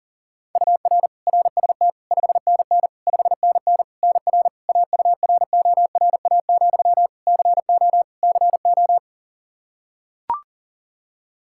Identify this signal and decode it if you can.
Morse code, heard as a keyed tone, about 38 wpm, 700 Hz. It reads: UR RST 5NN 5NN NR AURORA, CO CO
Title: Practice QSO - Exchange 1 - Beginning - 40wpm